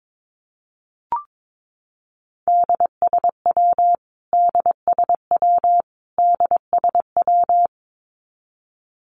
Morse code, heard as a keyed tone, about 22 wpm, 700 Hz. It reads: DSW DSW DSW
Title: Abbreviations Repeated 3x - 22wpm